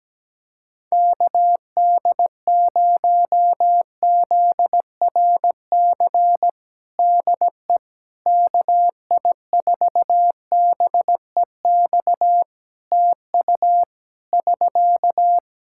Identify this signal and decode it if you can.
Morse code, heard as a keyed tone, about 17 wpm, 700 Hz. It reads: KD0ZRC DE KI4BEX TU <SK>
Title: Practice QSO - Exchange 4 - Ending - 17wpm